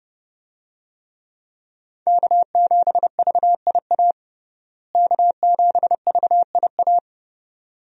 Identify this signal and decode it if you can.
Morse code, heard as a keyed tone, about 30 wpm, 700 Hz. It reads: K7VIA K7VIA
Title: USA Call Signs - 5 Characters - Repeated 2x - 30wpm